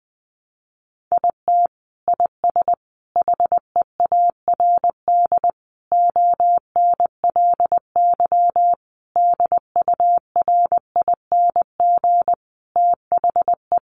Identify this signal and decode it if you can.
Morse code, heard as a keyed tone, about 20 wpm, 700 Hz. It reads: IT IS HEARD ONLY DURING THE
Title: Sentences from Top 1200 Words 20wpm